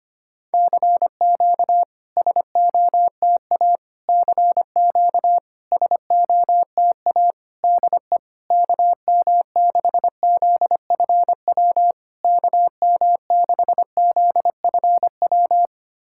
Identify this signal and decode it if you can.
Morse code, heard as a keyed tone, about 25 wpm, 700 Hz. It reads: CQ SOTA CQ SOTA DE KM6ZFW KM6ZFW
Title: SOTA QSO Practice 25wpm